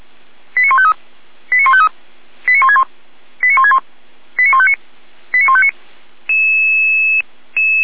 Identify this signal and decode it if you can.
Five-tone selective call (ZVEI), heard as quick runs of five tones, tones of about 70 ms. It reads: 89153 89153 89161 89161 89159 89159
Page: /455/SDR/ZVEI/